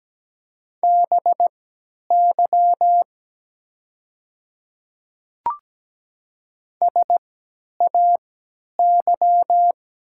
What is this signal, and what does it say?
Morse code, heard as a keyed tone, about 17 wpm, spaced as for 10 wpm, 700 Hz. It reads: BY SAY
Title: 3x Character Spacing - Top 300 Words - 17wpm